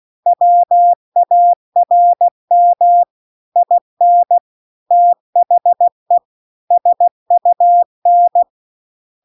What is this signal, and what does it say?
Morse code, heard as a keyed tone, about 16 wpm, 700 Hz. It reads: WARM IN THE SUN
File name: CW-The-Big-Twoheaded-River-2-16WPM